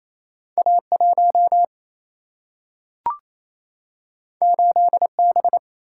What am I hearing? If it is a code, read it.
Morse code, heard as a keyed tone, about 28 wpm, 700 Hz. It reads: A 1 8 B
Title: Single-Letter Single-Number 28wpm